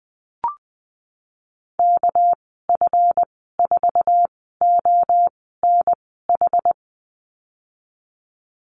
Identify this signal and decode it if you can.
Morse code, heard as a keyed tone, about 19 wpm, spaced as for 14 wpm, 700 Hz. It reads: KF4ONH
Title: 219 Lesson 35g - Number 0 - Call Signs 20wpm